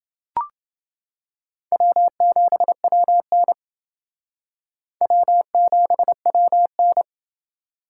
Morse code, heard as a keyed tone, about 28 wpm, 700 Hz. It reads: W7WN W7WN